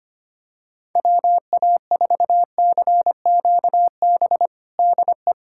Morse code, heard as a keyed tone, about 25 wpm, 700 Hz. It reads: WA4CQB DE
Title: USA Callsign DE Callsign - 6 Characters - 25wpm